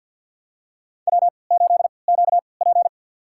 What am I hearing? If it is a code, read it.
Morse code, heard as a keyed tone, about 53 wpm, spaced as for 29 wpm, 700 Hz. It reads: W8QP
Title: USA Call Signs - 4 Characters - 3x Character Spacing - 50wpm